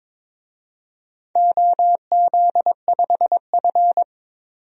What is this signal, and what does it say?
Morse code, heard as a keyed tone, about 22 wpm, 700 Hz. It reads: OZ5F
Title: DXCC-100 Call Signs - 4 Characters - 22wpm